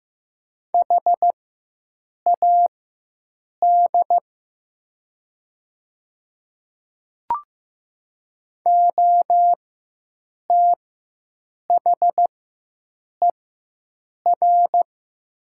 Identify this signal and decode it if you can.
Morse code, heard as a keyed tone, about 15 wpm, spaced as for 7 wpm, 700 Hz. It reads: HAD OTHER
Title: 4x Character Spacing - Top 100 Words - 15wpm